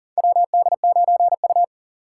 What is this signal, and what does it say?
Morse code, heard as a keyed tone, about 38 wpm, 700 Hz. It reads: WD9U
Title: USA Call Signs - 4 Characters - 3x Character Spacing - 40wpm